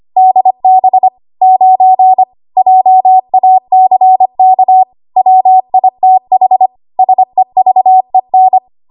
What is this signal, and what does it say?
Morse code, heard as a keyed tone, about 25 wpm, 750 Hz. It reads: DB 9 JACK WITH SEVEN
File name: Archive / 25 WPM / 25WPM